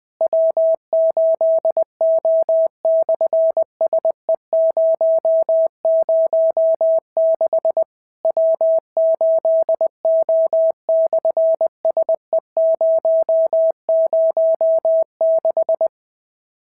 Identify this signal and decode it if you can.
Morse code, heard as a keyed tone, about 20 wpm, 650 Hz. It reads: W8O/SE006 W8O/SE006
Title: SOTA QSO Practice 20wpm